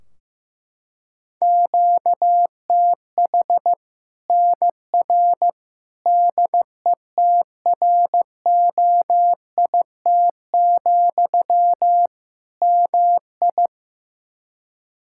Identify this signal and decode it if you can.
Morse code, heard as a keyed tone, about 14 wpm, 700 Hz. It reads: QTH NR DETROIT, MI
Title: K1USN SST QSO Practice 15wpm